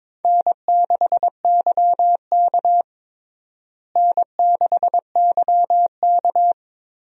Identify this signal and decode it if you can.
Morse code, heard as a keyed tone, about 22 wpm, 700 Hz. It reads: N6YK N6YK
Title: USA Call Signs - 4 Characters - Repeated 3x - 22wpm